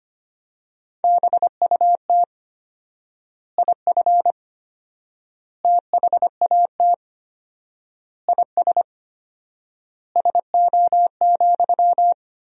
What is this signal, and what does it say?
Morse code, heard as a keyed tone, about 25 wpm, 700 Hz. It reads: BUT IF THAT IS SO,
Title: 4x Word-Spacing - Sentences from Top 200 Words - 25wpm